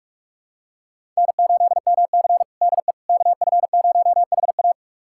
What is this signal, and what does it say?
Morse code, heard as a keyed tone, about 41 wpm, 700 Hz. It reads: N8MC DE KF0HA
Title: USA Callsign DE Callsign 45wpm